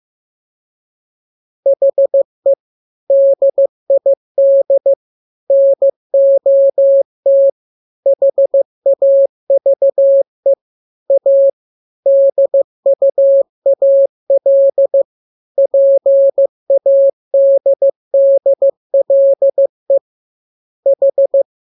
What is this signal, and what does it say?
Morse code, heard as a keyed tone, about 15 wpm, 550 Hz. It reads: HE DID NOT HAVE A DUAL PADDLE H
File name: jokes / joke 1 15 0000